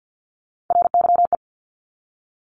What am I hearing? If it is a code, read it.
Morse code, heard as a keyed tone, about 50 wpm, 700 Hz. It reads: RYE
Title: Multi-letter Phonemes Repeated 3x 50wpm